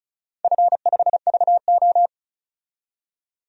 Morse code, heard as a keyed tone, about 36 wpm, 700 Hz. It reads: F5VO
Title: DXCC-100 Call Signs - 4 Characters - 35wpm